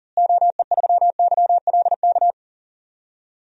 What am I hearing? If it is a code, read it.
Morse code, heard as a keyed tone, about 40 wpm, 700 Hz. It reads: OE3YLK
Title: DXCC-100 Call Signs - 6 Characters - Repeated 3x - 40wpm